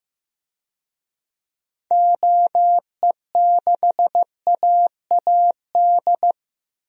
Morse code, heard as a keyed tone, about 15 wpm, 700 Hz. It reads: OE6AAD
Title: DXCC-100 Call Signs - 6 Characters - 15wpm